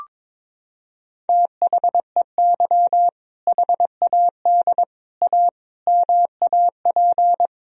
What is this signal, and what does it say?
Morse code, heard as a keyed tone, about 22 wpm, 700 Hz. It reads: THEY HAD A MAP
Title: Sentences from Top 1000 Words - Review Part 2 - 22wpm